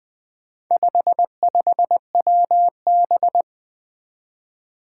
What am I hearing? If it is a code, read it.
Morse code, heard as a keyed tone, about 20 wpm, 700 Hz. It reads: 55WB
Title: Four Challenging Letter-Number Combinations 20wpm